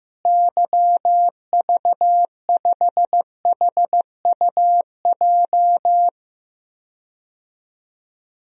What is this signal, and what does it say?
Morse code, heard as a keyed tone, about 15 wpm, 700 Hz. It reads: YV5HUJ
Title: DXCC-100 Call Signs - 6 Characters - Repeated 2x - 15wpm